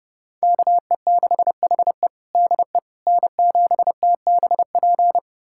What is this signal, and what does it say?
Morse code, heard as a keyed tone, about 32 wpm, 700 Hz. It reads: KE6HE DE N7TBP
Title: USA Callsign DE Callsign - 5 Characters - 30wpm